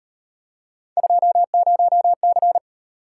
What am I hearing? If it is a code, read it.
Morse code, heard as a keyed tone, about 37 wpm, 700 Hz. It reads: 20C